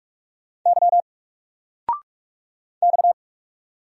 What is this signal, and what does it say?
Morse code, heard as a keyed tone, about 46 wpm, 700 Hz. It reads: Y X
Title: Single Letter-Number Alt 45wpm